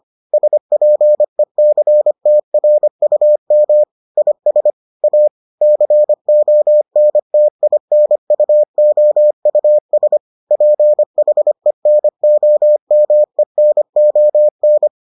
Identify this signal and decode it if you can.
Morse code, heard as a keyed tone, about 25 wpm, 600 Hz. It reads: SPECTRUM IS A CONTINUOUS PHENOMENON